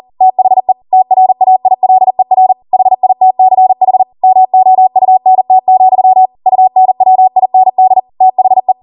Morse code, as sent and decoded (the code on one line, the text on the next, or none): - .... / - .-. .- .. .-.. .-. / .... .. - -.-. .... / -- --- ..- -. - --..-- / ..- -. .-- .. -. -.. / - .... .
THE TRAILER HITCH MOUNT, UNWIND THE